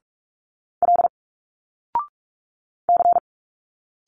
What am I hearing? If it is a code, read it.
Morse code, heard as a keyed tone, about 46 wpm, 700 Hz. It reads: L C
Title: Single Letters 45wpm